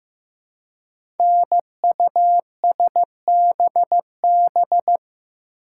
Morse code, heard as a keyed tone, about 15 wpm, 700 Hz. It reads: NUSBB
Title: Code Groups - 5 Letters - 15wpm